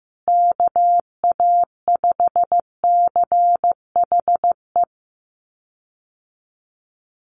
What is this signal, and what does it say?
Morse code, heard as a keyed tone, about 15 wpm, 700 Hz. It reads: KA5CHE
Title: USA Call Signs Repeated 3x 15wpm